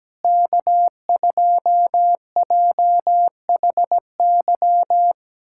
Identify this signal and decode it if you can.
Morse code, heard as a keyed tone, about 17 wpm, 700 Hz. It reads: K2JHY
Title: USA Callsign DE Callsign - 5 Characters - 17wpm